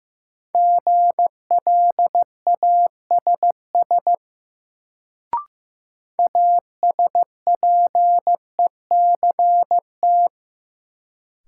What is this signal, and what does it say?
Morse code, heard as a keyed tone, about 15 wpm, 700 Hz. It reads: GLASS ASPECT
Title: Sentences from Top 1300 Words - New Word Drill Alt - 15wpm